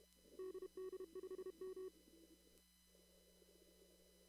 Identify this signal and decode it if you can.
Morse code, heard as a keyed tone, about 32 wpm, 370 Hz. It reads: DD5M
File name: CW / kt5j wpx cw-op-k5pi